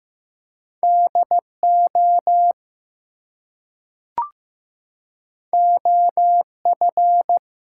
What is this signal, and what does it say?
Morse code, heard as a keyed tone, about 15 wpm, 700 Hz. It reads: DO OF
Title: Two Letter Words - 15wpm